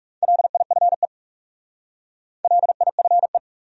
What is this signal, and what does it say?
Morse code, heard as a keyed tone, about 44 wpm, 700 Hz. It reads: LIFE LIFE